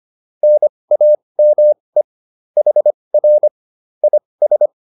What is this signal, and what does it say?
Morse code, heard as a keyed tone, about 25 wpm, spaced as for 20 wpm, 600 Hz. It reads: NAME HR IS